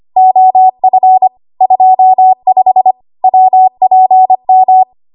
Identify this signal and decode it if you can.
Morse code, heard as a keyed tone, about 25 wpm, 750 Hz. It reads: OF 25 WPM
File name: Archive / 25 WPM / 25WPM